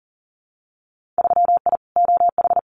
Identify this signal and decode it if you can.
Morse code, heard as a keyed tone, about 37 wpm, 700 Hz. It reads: 3I OH
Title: Winter Field Day Exchanges Encore 40wpm